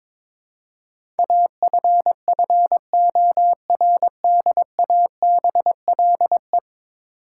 Morse code, heard as a keyed tone, about 22 wpm, 700 Hz. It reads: AFFORDABLE
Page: morse-code-ninja-practice-22wpm